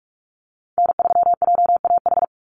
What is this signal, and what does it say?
Morse code, heard as a keyed tone, about 44 wpm, 700 Hz. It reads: N3JAH
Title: USA Call Signs 45wpm